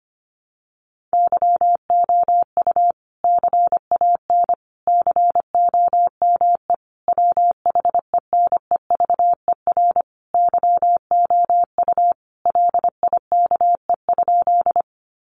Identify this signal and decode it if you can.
Morse code, heard as a keyed tone, about 25 wpm, 700 Hz. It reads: YOU CAN COME WHENEVER YOU LIKE?
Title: Sentences from Top 1200 Words 25wpm